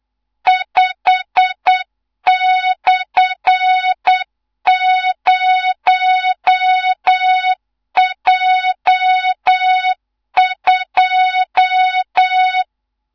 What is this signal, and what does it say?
Morse code, heard as a keyed tone, about 8 wpm, 3700 Hz, with noise. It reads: 5/0J2
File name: Morze-HA2MN / hang / 15 GYAKORLAT